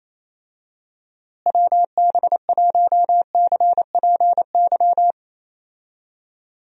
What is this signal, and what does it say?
Morse code, heard as a keyed tone, about 28 wpm, 700 Hz. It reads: WB1CPY
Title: USA Call Signs - 6 Characters - Repeated 2x - 28wpm